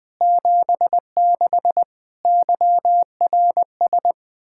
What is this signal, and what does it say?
Morse code, heard as a keyed tone, about 20 wpm, 700 Hz. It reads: 76 YRS
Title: Practice QSO - Exchange 3 - More Info - 20wpm